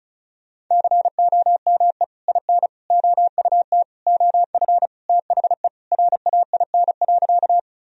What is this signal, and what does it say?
Morse code, heard as a keyed tone, about 36 wpm, 700 Hz. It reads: COME IN OUT OF THE RAIN.
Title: Sentences from Top 1000 Words - Review Part 1 - 35wpm